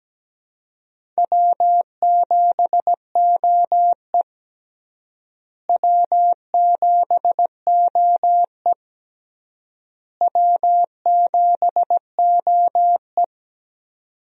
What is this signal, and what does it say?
Morse code, heard as a keyed tone, about 17 wpm, 700 Hz. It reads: W7OE W7OE W7OE
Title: USA Call Signs - 4 Characters - Repeated 3x - 17wpm